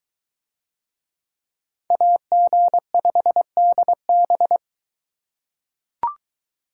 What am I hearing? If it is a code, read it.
Morse code, heard as a keyed tone, about 22 wpm, 700 Hz. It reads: AG5DB